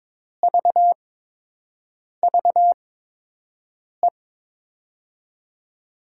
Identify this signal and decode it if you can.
Morse code, heard as a keyed tone, about 21 wpm, spaced as for 6 wpm, 700 Hz. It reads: VVE
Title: USA Call Signs - 6 Characters - 8x Character Spacing - 22wpm